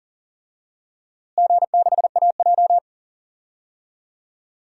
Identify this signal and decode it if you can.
Morse code, heard as a keyed tone, about 40 wpm, 700 Hz. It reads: G6AJ
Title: DXCC-100 Call Signs - 4 Characters - Repeated 2x - 40wpm